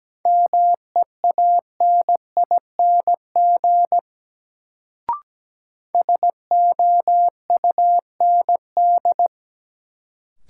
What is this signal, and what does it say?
Morse code, heard as a keyed tone, about 17 wpm, 700 Hz. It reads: MEANING SOUND
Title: Sentences from Top 900 Words - New Word Drill - 17wpm